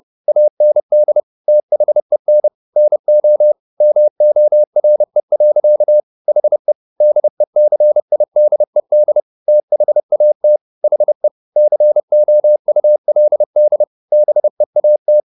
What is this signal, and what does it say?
Morse code, heard as a keyed tone, about 30 wpm, 600 Hz. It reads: AND THEN NO MORE. HE DECIDED THAT HE COULD BEAT
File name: the old man and the sea 2 30wpm